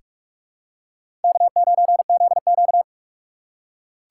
Morse code, heard as a keyed tone, about 46 wpm, 700 Hz. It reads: K9ZQ
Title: USA Call Signs - 4 Characters - 2x Character Spacing - 45wpm